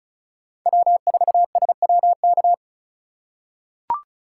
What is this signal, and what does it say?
Morse code, heard as a keyed tone, about 36 wpm, 700 Hz. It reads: W4SWK
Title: USA Call Signs - 2x Character Spacing - 35wpm